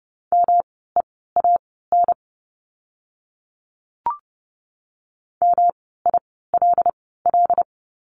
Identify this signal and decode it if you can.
Morse code, heard as a keyed tone, about 30 wpm, spaced as for 17 wpm, 700 Hz. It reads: MEAN MILL
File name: course / 30:10wpm / 108 Lesson 18f - Letter m - Words 30wpm